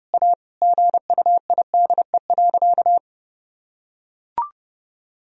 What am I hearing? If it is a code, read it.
Morse code, heard as a keyed tone, about 30 wpm, 700 Hz. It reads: A GUIDE.
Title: Sentences from Top 1000 Words - Review Part 3 - 30wpm